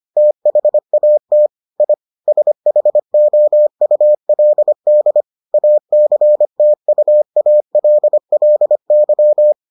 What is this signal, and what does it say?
Morse code, heard as a keyed tone, about 25 wpm, 600 Hz. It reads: THAT I SHOULD ACTUALLY